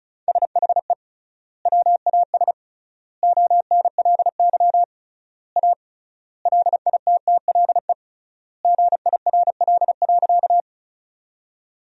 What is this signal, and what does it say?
Morse code, heard as a keyed tone, about 34 wpm, 700 Hz. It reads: SHE WAS ONLY A LITTLE GIRL.
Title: 3x Word-Spacing - Sentences from Top 400 Words - 35wpm